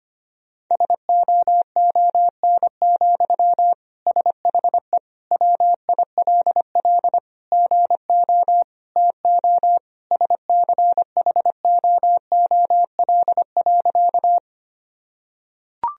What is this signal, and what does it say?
Morse code, heard as a keyed tone, about 25 wpm, 700 Hz. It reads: SOON, SHE WILL GO TO SCHOOL.
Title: Sentences from Top 1000 Words - Review Part 1 - 25wpm